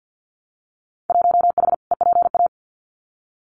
Morse code, heard as a keyed tone, about 50 wpm, 700 Hz. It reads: 1H EPA
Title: Winter Field Day Exchanges 50wpm